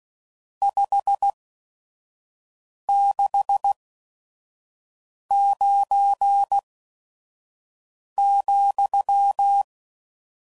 Morse code, heard as a keyed tone, about 16 wpm, spaced as for 5 wpm, 800 Hz. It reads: 569,
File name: CW Code Groups @5WPM / RST's / Various RST No's